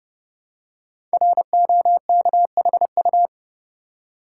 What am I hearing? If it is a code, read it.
Morse code, heard as a keyed tone, about 30 wpm, 700 Hz. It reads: ROKHU
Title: Code Groups - 5 Letters - 30wpm